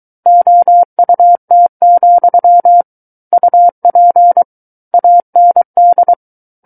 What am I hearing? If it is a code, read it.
Morse code, heard as a keyed tone, about 23 wpm, 700 Hz. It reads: OUT, UP AND
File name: Book One / 14-Chapter 14 - 23 WPM